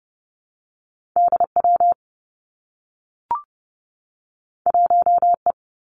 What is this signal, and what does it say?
Morse code, heard as a keyed tone, about 30 wpm, 700 Hz. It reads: D W 1 E
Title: Two Challenging Letter-Number Combinations 30wpm